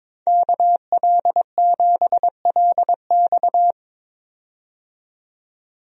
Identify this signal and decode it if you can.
Morse code, heard as a keyed tone, about 22 wpm, 700 Hz. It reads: KL7LX
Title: DXCC-100 Call Signs - 5 Characters - 22wpm